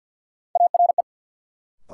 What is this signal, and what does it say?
Morse code, heard as a keyed tone, about 53 wpm, 700 Hz. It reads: ARE
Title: Sentences from Top 100 Words - New Word Drill - 50wpm